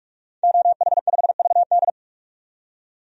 Morse code, heard as a keyed tone, about 46 wpm, 700 Hz. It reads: OH5VD
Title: DXCC-100 Call Signs - 5 Characters - 45wpm